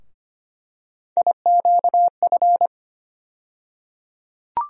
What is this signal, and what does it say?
Morse code, heard as a keyed tone, about 25 wpm, 700 Hz. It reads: IQF
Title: USA Call Sign Postfixes - 3 Letters 25wpm